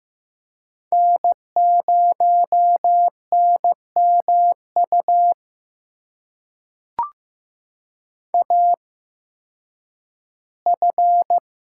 Morse code, heard as a keyed tone, about 15 wpm, spaced as for 10 wpm, 700 Hz. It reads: N0NMU AF
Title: USA Call Signs - 8x Character Spacing - 15wpm